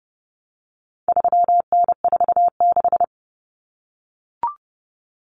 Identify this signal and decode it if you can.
Morse code, heard as a keyed tone, about 30 wpm, 700 Hz. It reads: 3N46
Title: Common Cut Numbers - 4-Digit 30wpm